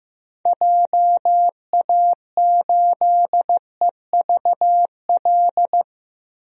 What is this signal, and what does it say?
Morse code, heard as a keyed tone, about 15 wpm, 700 Hz. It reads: JA8EVL
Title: DXCC-100 Call Signs - 6 Characters - 15wpm